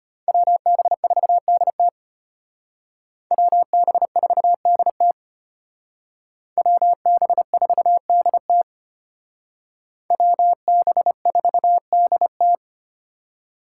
Morse code, heard as a keyed tone, about 33 wpm, 700 Hz. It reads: WB4DT WB4DT WB4DT WB4DT